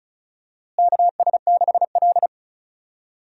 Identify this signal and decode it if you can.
Morse code, heard as a keyed tone, about 36 wpm, 700 Hz. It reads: KS6L
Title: USA Call Signs - 4 Characters - Repeated 2x - 35wpm